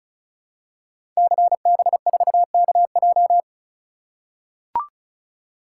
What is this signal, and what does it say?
Morse code, heard as a keyed tone, about 36 wpm, 700 Hz. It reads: CB4KJ
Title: Code Groups - 5 Letter:Numbers - 35wpm